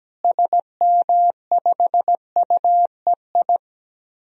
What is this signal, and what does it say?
Morse code, heard as a keyed tone, about 17 wpm, 700 Hz. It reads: SM5UEI
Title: DXCC-100 Call Signs - 6 Characters - Repeated 2x - 17wpm